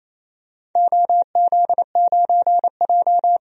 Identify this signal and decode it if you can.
Morse code, heard as a keyed tone, about 28 wpm, 700 Hz. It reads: OZ9J
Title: DXCC-100 Call Signs - 4 Characters - 28wpm